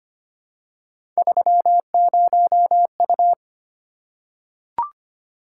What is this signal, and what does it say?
Morse code, heard as a keyed tone, about 25 wpm, 700 Hz. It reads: 30U
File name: Three Letter-Number Combinations - 25wpm